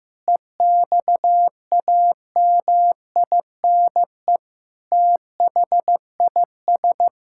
Morse code, heard as a keyed tone, about 15 wpm, 700 Hz. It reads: EXAMINE THIS
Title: Sentences from Top 1600 Words 15wpm